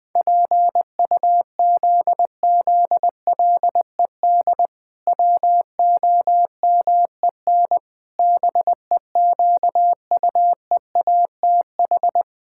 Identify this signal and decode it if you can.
Morse code, heard as a keyed tone, about 20 wpm, 700 Hz. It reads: PUZZLED WOMEN BEQUEATH
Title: Pangram Challenge v2 - 20wpm